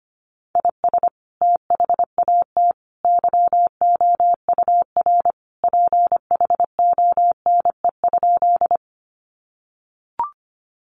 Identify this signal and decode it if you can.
Morse code, heard as a keyed tone, about 25 wpm, 700 Hz. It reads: IS THAT YOUR PHONE?
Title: Sentences from Top 1100 Words 25wpm